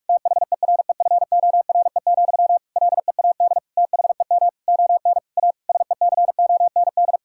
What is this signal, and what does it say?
Morse code, heard as a keyed tone, about 43 wpm, 700 Hz. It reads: THEREFORE, LEAD THEM ON A SECOND
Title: Sentences from Top 1000 Words - Review Part 5 - 45wpm